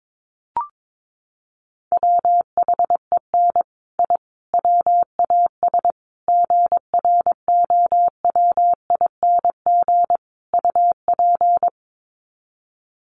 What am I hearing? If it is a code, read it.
Morse code, heard as a keyed tone, about 22 wpm, 700 Hz. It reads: WHEN I WAS GROWING UP
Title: Sets of 5 Words Encore 22wpm